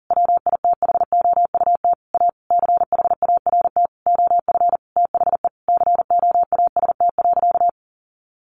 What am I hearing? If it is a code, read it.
Morse code, heard as a keyed tone, about 40 wpm, 700 Hz. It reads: WITHOUT A CHART OF THE COAST.
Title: Sentences from Top 1300 Words 40wpm